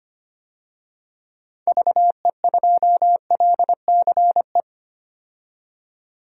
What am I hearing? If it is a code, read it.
Morse code, heard as a keyed tone, about 25 wpm, 700 Hz. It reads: VE2LCE
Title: DXCC-100 Call Signs - 6 Characters - 25wpm